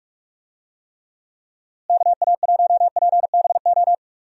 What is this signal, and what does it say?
Morse code, heard as a keyed tone, about 46 wpm, 700 Hz. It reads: KA1PBO
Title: USA Call Signs - 6 Characters - Repeated 2x - 45wpm